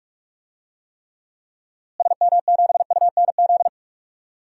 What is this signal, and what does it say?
Morse code, heard as a keyed tone, about 47 wpm, 700 Hz. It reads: SM7UNZ
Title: DXCC-100 Call Signs - 6 Characters - Repeated 2x - 45wpm